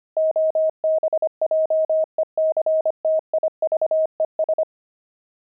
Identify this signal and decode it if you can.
Morse code, heard as a keyed tone, about 25 wpm, 630 Hz. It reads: OBJECTIVES
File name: pangram-fs-25-cs-25